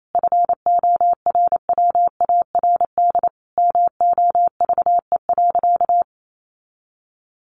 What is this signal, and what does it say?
Morse code, heard as a keyed tone, about 28 wpm, 700 Hz. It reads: FORWARD MOVE.
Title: Sentences from Top 1300 Words 28wpm